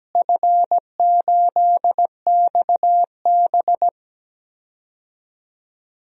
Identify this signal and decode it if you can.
Morse code, heard as a keyed tone, about 17 wpm, 700 Hz. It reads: F8XB
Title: DXCC-100 Call Signs - 4 Characters - Repeated 3x - 17wpm